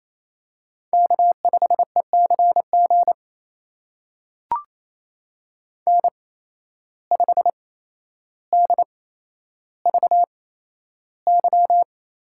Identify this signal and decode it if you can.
Morse code, heard as a keyed tone, about 28 wpm, spaced as for 12 wpm, 700 Hz. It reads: K5ECG N5DVY
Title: USA Call Signs - 5 Characters - 8x Character Spacing - 28wpm